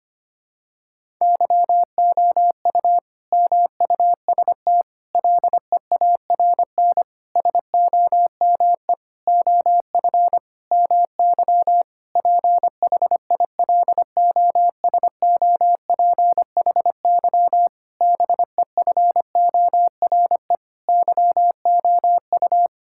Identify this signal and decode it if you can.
Morse code, heard as a keyed tone, about 25 wpm, 700 Hz. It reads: YOU MUST LEARN SOME OF MY PHILOSOPHY BEFORE YOU
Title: Sentences from Top 1000 Words - Review Part 6 - 25wpm